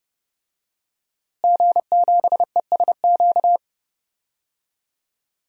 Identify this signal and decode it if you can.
Morse code, heard as a keyed tone, about 30 wpm, 700 Hz. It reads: G7ESQ
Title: DXCC-100 Call Signs - 5 Characters - Repeated 3x - 30wpm